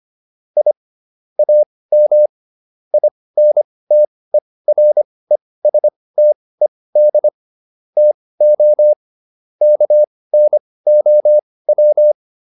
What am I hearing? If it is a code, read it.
Morse code, heard as a keyed tone, about 25 wpm, spaced as for 18 wpm, 600 Hz. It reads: I AM INTERESTED TO KNOW
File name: QSO409 18